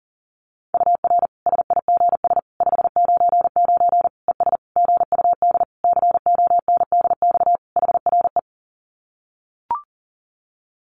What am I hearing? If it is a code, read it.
Morse code, heard as a keyed tone, about 40 wpm, 700 Hz. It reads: UR SIGS 599 ES GUD CONDX HRE
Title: Common QSO Phrases 40wpm